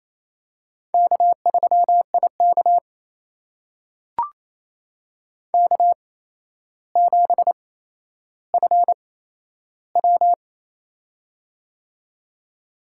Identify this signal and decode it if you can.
Morse code, heard as a keyed tone, about 28 wpm, spaced as for 12 wpm, 700 Hz. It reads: K3IK K7FW
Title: USA Call Signs - 4 Characters - 8x Character Spacing - 28wpm